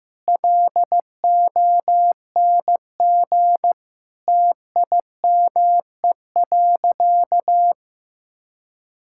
Morse code, heard as a keyed tone, about 15 wpm, 700 Hz. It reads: LONG TIME.
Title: Sentences from Top 1000 Words - Review Part 6 - 15wpm